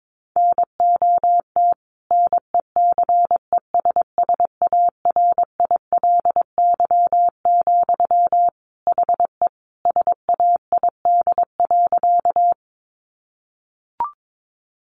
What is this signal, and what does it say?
Morse code, heard as a keyed tone, about 22 wpm, 700 Hz. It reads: NOT NECESSARILY, HE SAID.
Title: Sentences from Top 1400 Words 22wpm